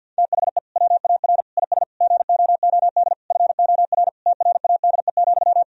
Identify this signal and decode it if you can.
Morse code, heard as a keyed tone, about 50 wpm, 700 Hz. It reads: THE WAR IS GOOD FOR TRADE,
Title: Sentences from Top 1000 Words - Review Part 4 - 50wpm